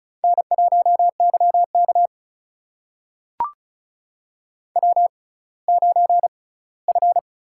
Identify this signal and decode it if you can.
Morse code, heard as a keyed tone, about 36 wpm, spaced as for 20 wpm, 700 Hz. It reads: N1YK W9F
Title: USA Call Signs - 4 Characters - 6x Character Spacing - 35wpm